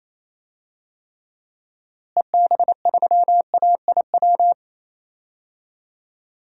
Morse code, heard as a keyed tone, about 28 wpm, 700 Hz. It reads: EB3AIW
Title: DXCC-100 Call Signs - 6 Characters - Repeated 3x - 28wpm